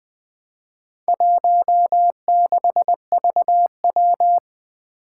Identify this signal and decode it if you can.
Morse code, heard as a keyed tone, about 20 wpm, 700 Hz. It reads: 16VW
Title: Four Challenging Letter-Number Combinations 20wpm